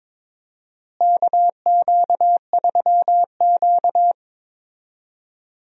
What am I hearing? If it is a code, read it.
Morse code, heard as a keyed tone, about 22 wpm, 700 Hz. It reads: KQ3Q
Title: USA Call Signs - 4 Characters - Repeated 3x - 22wpm